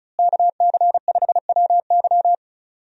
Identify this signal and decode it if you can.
Morse code, heard as a keyed tone, about 36 wpm, 700 Hz. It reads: KC5WY
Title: USA Call Signs - 5 Characters - Repeated 3x - 35wpm